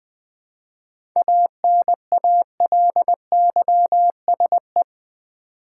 Morse code, heard as a keyed tone, about 20 wpm, 700 Hz. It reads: ANALYSE
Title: 7 Letter Words - First 3 Letters Given - 20wpm